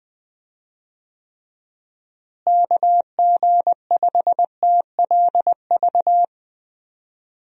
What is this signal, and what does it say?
Morse code, heard as a keyed tone, about 20 wpm, 700 Hz. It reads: KG5TLV
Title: USA Call Signs - 6 Characters - Repeated 3x - 20wpm